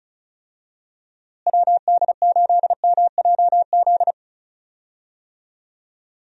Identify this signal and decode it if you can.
Morse code, heard as a keyed tone, about 36 wpm, 700 Hz. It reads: WD8MJZ
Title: USA Call Signs - 6 Characters - Repeated 3x - 35wpm